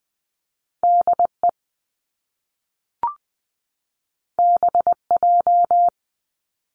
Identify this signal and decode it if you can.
Morse code, heard as a keyed tone, about 20 wpm, 700 Hz. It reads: DE BJ
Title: Top Two Letter Combinations in Words 20wpm